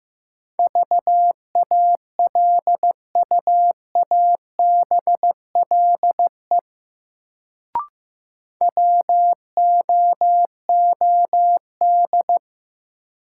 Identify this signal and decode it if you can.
Morse code, heard as a keyed tone, about 16 wpm, 700 Hz. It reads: VALUABLE WOOD
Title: Sentences from Top 1100 Words - New Word Drill Alt - 15wpm